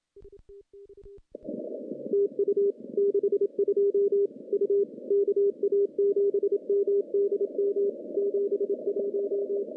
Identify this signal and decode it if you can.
Morse code, heard as a keyed tone, about 26 wpm, 400 Hz. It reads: TU 62 U KA7MDM 71
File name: contests / CW-SS-k5tr- -0400